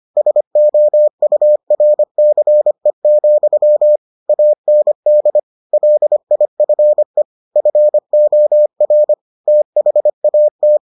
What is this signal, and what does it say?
Morse code, heard as a keyed tone, about 25 wpm, 600 Hz. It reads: SOURCE, AND LIFE FOR THAT